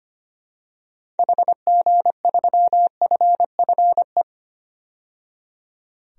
Morse code, heard as a keyed tone, about 25 wpm, 700 Hz. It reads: HG3FFE
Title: DXCC-100 Call Signs - 6 Characters - 25wpm